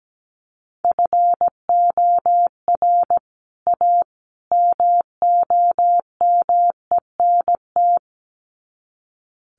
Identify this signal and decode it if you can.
Morse code, heard as a keyed tone, about 17 wpm, 700 Hz. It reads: FOR A MOMENT
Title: Sets of 3 Words 17wpm